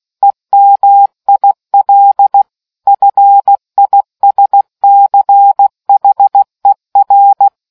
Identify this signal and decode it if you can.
Morse code, heard as a keyed tone, about 16 wpm, 800 Hz. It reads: EMIL FISCHER